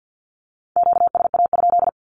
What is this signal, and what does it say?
Morse code, heard as a keyed tone, about 48 wpm, 700 Hz. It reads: QSA?
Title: Q-Codes 50wpm